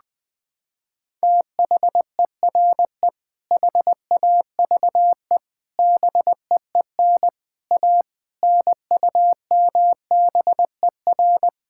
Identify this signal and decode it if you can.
Morse code, heard as a keyed tone, about 20 wpm, 700 Hz. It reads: THERE HAVE BEEN A NUMBER
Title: Sets of 6 Words - 20wpm